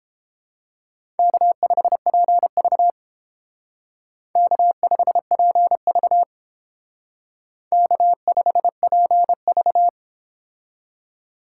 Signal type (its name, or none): Morse code